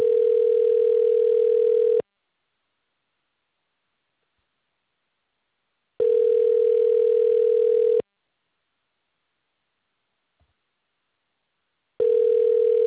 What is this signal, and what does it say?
Ringback tone (North America), 440 + 480 Hz.